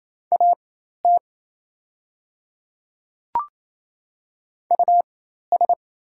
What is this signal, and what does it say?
Morse code, heard as a keyed tone, about 27 wpm, spaced as for 13 wpm, 690 Hz. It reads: AT US
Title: 4x Character Spacing - Top 300 Words - 28wpm